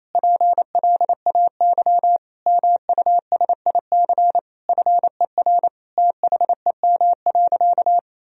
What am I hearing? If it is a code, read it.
Morse code, heard as a keyed tone, about 28 wpm, 700 Hz. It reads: PLAY MUSIC FER THEM.
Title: Sentences from Top 300 Words and Abbreviations 28wpm